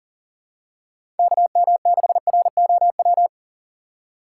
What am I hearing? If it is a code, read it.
Morse code, heard as a keyed tone, about 40 wpm, 700 Hz. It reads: KM6ROW
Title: USA Call Signs - 6 Characters - 3x Character Spacing - 40wpm